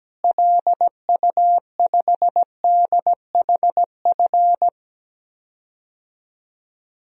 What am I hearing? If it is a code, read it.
Morse code, heard as a keyed tone, about 17 wpm, 700 Hz. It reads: LU5DHF
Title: DXCC-100 Call Signs - 6 Characters - Repeated 3x - 17wpm